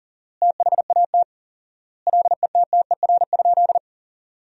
Morse code, heard as a keyed tone, about 37 wpm, 700 Hz. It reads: THAT LETTER?
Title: 4x Word-Spacing - Sentences from Top 300 Words - 40wpm